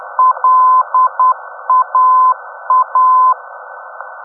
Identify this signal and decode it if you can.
Morse code, heard as a keyed tone, about 10 wpm, 1020 Hz. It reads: LAA
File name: europe / d-5078- dec 03 - ndb laa-ti